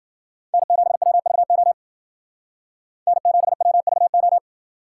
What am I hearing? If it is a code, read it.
Morse code, heard as a keyed tone, about 48 wpm, 700 Hz. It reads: N7WVO N7WVO
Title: Speed Racing - USA 1x3 Call Signs 40wpm